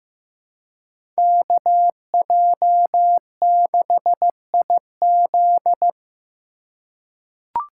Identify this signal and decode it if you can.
Morse code, heard as a keyed tone, about 15 wpm, 700 Hz. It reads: KJ6IZ
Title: USA Call Signs - 5 Characters - 3x Character Spacing - 15wpm